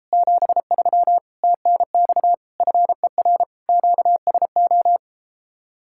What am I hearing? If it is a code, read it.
Morse code, heard as a keyed tone, about 33 wpm, 700 Hz. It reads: 73 TNX FER QSO